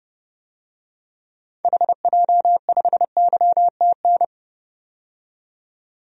Morse code, heard as a keyed tone, about 30 wpm, 700 Hz. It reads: HJ5YTN
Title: DXCC-100 Call Signs - 6 Characters - 30wpm